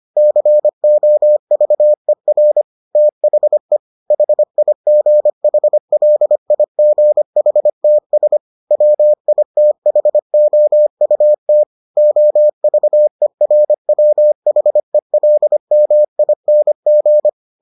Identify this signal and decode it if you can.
Morse code, heard as a keyed tone, about 25 wpm, 600 Hz. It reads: COVER THE HIGHLIGHTS WITHOUT OVERWHELMING